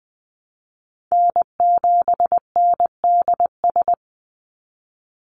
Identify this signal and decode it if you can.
Morse code, heard as a keyed tone, about 20 wpm, 700 Hz. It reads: N7NDS
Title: USA Call Signs 20wpm